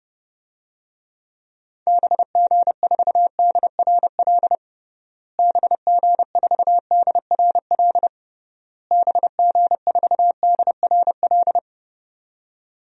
Morse code, heard as a keyed tone, about 32 wpm, 700 Hz. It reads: BG4DRL BG4DRL BG4DRL
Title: DXCC-100 Call Signs - 6 Characters - Repeated 3x - 30wpm